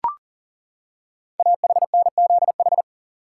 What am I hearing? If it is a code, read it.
Morse code, heard as a keyed tone, about 40 wpm, 700 Hz. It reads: AHNZH
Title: Code Groups - 5 Letters - 40wpm